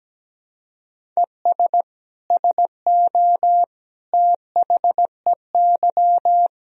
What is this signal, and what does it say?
Morse code, heard as a keyed tone, about 17 wpm, 700 Hz. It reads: ES SO THEY
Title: Sentences from Top 400 Words and Abbreviations 17wpm